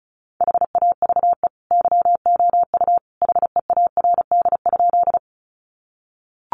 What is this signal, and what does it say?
Morse code, heard as a keyed tone, about 34 wpm, 700 Hz. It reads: HAVE YOU HEARD?
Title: Sentences from Top 400 Words 35wpm